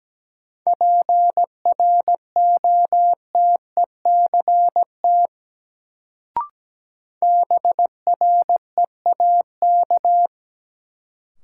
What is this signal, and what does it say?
Morse code, heard as a keyed tone, about 17 wpm, 700 Hz. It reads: PROTECT BREAK
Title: Sentences from Top 900 Words - New Word Drill - 17wpm